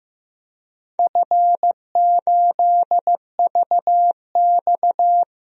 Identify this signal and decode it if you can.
Morse code, heard as a keyed tone, about 15 wpm, 700 Hz. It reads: F8VX
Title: DXCC-100 Call Signs - 4 Characters - 15wpm